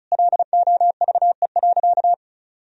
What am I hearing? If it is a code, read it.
Morse code, heard as a keyed tone, about 36 wpm, 700 Hz. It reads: LOVE.
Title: Sentences from Top 1000 Words - Review Part 2 - 35wpm